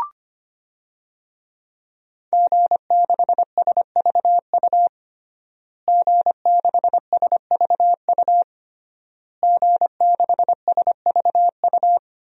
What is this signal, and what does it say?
Morse code, heard as a keyed tone, about 25 wpm, 700 Hz. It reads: G6SVU G6SVU G6SVU
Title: DXCC-100 Call Signs - 5 Characters - Repeated 3x - 25wpm